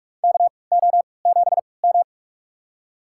Morse code, heard as a keyed tone, about 46 wpm, spaced as for 25 wpm, 700 Hz. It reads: KO7M
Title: USA Call Signs - 4 Characters - 3x Character Spacing - 45wpm